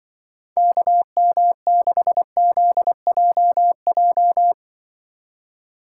Morse code, heard as a keyed tone, about 24 wpm, 700 Hz. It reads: KM6ZJJ
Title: Speed Racing - USA 2x3 Call Signs 20wpm